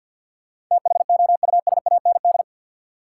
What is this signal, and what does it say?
Morse code, heard as a keyed tone, about 50 wpm, 700 Hz. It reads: THOUSAND
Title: Sentences from Top 600 Words - New Word Drill - 50wpm